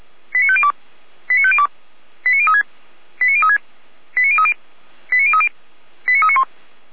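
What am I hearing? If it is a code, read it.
Five-tone selective call (ZVEI), heard as quick runs of five tones, tones of about 70 ms. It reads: 89502 89502 89036 89036 89030 89030 89391
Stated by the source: Five-tone ZVEI selective call received on an SDR.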